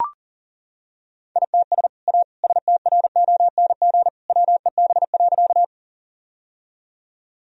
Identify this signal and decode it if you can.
Morse code, heard as a keyed tone, about 40 wpm, 700 Hz. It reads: ITS A STRONG WEB.
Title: Sentences from Top 1000 Words - Review Part 4 - 40wpm